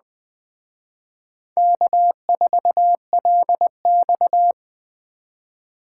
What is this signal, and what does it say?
Morse code, heard as a keyed tone, about 21 wpm, 700 Hz. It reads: K4LX
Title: Speed Racing - USA 1x2 Call Signs 17wpm